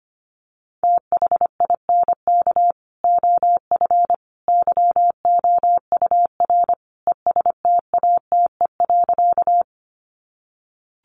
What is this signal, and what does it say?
Morse code, heard as a keyed tone, about 25 wpm, 700 Hz. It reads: THINK OF YOUR ESTATE.
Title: Sentences from Top 1200 Words 25wpm